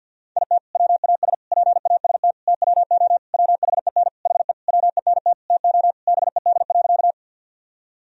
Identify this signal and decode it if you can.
Morse code, heard as a keyed tone, about 53 wpm, 700 Hz. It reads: IT WAS PAST TWO WHEN HE WENT TO BED.
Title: Sentences from Top 1000 Words - Review Part 3 - 50wpm